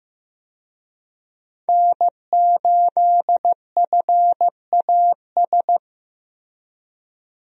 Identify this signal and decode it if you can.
Morse code, heard as a keyed tone, about 15 wpm, 700 Hz. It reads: N8FAS
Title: USA Call Signs - 5 Characters - Repeated 2x - 15wpm